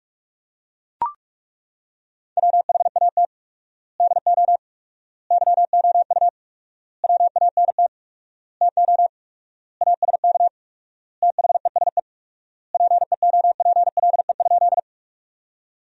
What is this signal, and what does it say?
Morse code, heard as a keyed tone, about 41 wpm, 700 Hz. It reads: WHAT DO YOU WANT TO ASK THESE PEOPLE?
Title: 4x Word-Spacing - Sentences from Top 200 Words - 45wpm